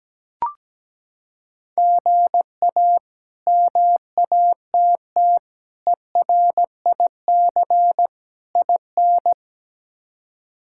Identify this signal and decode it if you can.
Morse code, heard as a keyed tone, about 17 wpm, 700 Hz. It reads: GA MATT ERIC IN
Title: K1USN SST QSO Practice - 17wpm